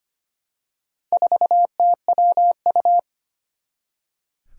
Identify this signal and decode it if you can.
Morse code, heard as a keyed tone, about 25 wpm, 700 Hz. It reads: 4TWU
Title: Four Challenging Letter-Number Combinations 25wpm